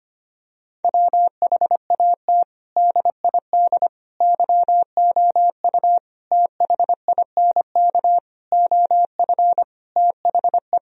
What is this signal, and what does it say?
Morse code, heard as a keyed tone, about 25 wpm, 700 Hz. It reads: WHAT DID YOU THINK OF THE
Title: Sentences from Top 1000 Words - Review Part 4 - 25wpm